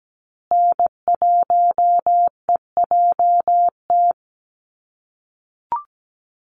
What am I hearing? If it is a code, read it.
Morse code, heard as a keyed tone, about 17 wpm, 700 Hz. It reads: N1EJT
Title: USA Call Signs Repeated 2x 17wpm